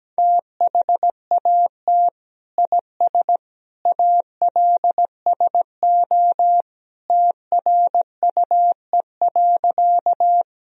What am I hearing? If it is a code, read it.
Morse code, heard as a keyed tone, about 17 wpm, 700 Hz. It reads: THAT IS ALSO TRUE.
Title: Sentences from Top 1000 Words - Review Part 1 - 17wpm